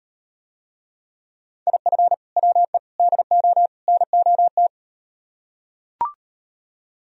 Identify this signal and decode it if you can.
Morse code, heard as a keyed tone, about 37 wpm, 700 Hz. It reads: IF WE DO NOT